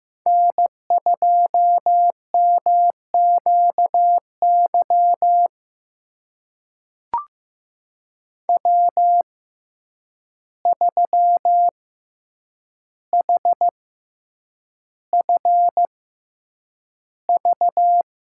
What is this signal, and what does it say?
Morse code, heard as a keyed tone, about 15 wpm, spaced as for 8 wpm, 700 Hz. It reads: N2MQY W3HFV
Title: USA Call Signs - 5 Characters - 6x Character Spacing - 15wpm